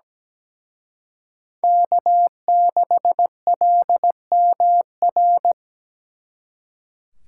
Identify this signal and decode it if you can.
Morse code, heard as a keyed tone, about 20 wpm, 700 Hz. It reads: K6LMR
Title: Speed Racing - USA Call Signs 17wpm